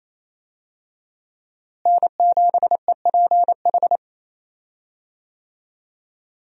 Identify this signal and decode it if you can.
Morse code, heard as a keyed tone, about 28 wpm, 700 Hz. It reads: N7EPH
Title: USA Call Signs - 5 Characters - Repeated 2x - 28wpm